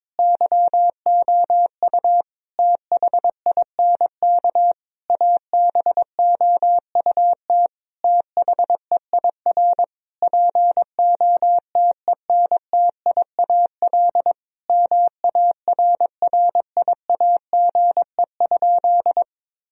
Morse code, heard as a keyed tone, about 22 wpm, 700 Hz. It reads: YOU THINK ABOUT THEIR POTENTIAL MARRIAGE?
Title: Sentences from Top 1000 Words - Review Part 4 - 22wpm